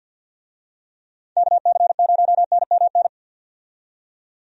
Morse code, heard as a keyed tone, about 47 wpm, 700 Hz. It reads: KC0NMN
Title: USA Call Signs - 6 Characters - 50wpm